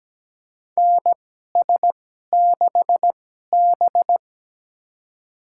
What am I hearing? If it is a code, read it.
Morse code, heard as a keyed tone, about 17 wpm, spaced as for 12 wpm, 700 Hz. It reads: NS6B
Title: USA Call Signs - 4 Characters - 2x Character Spacing - 17wpm